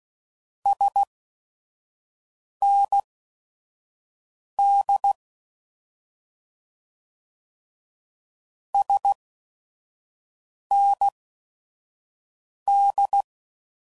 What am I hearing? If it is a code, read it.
Morse code, heard as a keyed tone, about 16 wpm, spaced as for 5 wpm, 800 Hz. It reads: SND SND
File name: Mixed / Abreviations / Abreviations